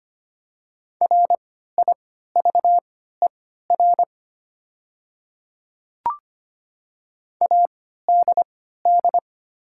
Morse code, heard as a keyed tone, about 25 wpm, spaced as for 14 wpm, 700 Hz. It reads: RIVER ADD
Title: 3x Character Spacing - Top 300 Words - 25wpm